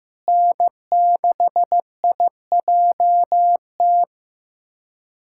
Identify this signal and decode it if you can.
Morse code, heard as a keyed tone, about 14 wpm, 690 Hz. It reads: N6IJT
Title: SOTA QSO Practice 15wpm